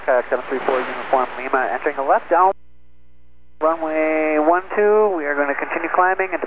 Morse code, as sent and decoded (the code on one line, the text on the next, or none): none